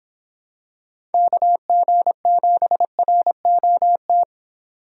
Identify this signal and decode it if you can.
Morse code, heard as a keyed tone, about 26 wpm, 700 Hz. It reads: KG7ROT